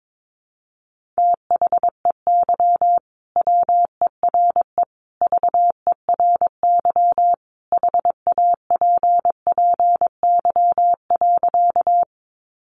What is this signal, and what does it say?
Morse code, heard as a keyed tone, about 22 wpm, 700 Hz. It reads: THEY WERE VERY HAPPY.
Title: Sentences from Top 1000 Words 22wpm